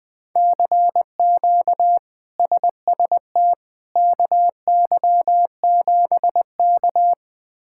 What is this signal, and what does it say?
Morse code, heard as a keyed tone, about 21 wpm, 700 Hz. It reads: CQ SST KY7K
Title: K1USN SST QSO Practice - 20wpm